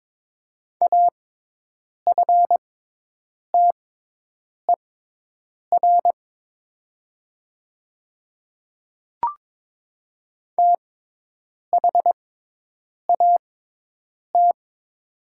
Morse code, heard as a keyed tone, about 22 wpm, spaced as for 8 wpm, 700 Hz. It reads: AFTER THAT
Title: 6x Character Spacing - Top 200 Words - 22wpm